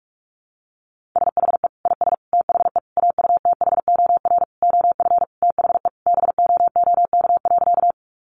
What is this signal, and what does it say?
Morse code, heard as a keyed tone, about 44 wpm, 700 Hz. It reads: SHE IS THE AUTHOR OF THE BOOK.
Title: Sentences from Top 1200 Words 45wpm